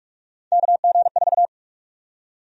Morse code, heard as a keyed tone, about 46 wpm, 700 Hz. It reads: KG4
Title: USA Call Sign Prefixes - 45wpm